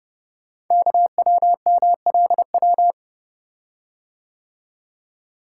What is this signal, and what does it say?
Morse code, heard as a keyed tone, about 29 wpm, 690 Hz. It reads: KWMLW
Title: Code Groups - 5 Letters - 30wpm